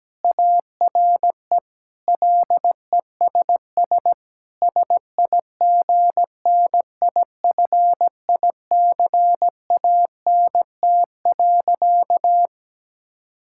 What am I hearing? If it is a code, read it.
Morse code, heard as a keyed tone, about 17 wpm, 700 Hz. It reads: ARE LESS SIGNIFICANT.
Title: Sentences from Top 1000 Words - Review Part 6 - 17wpm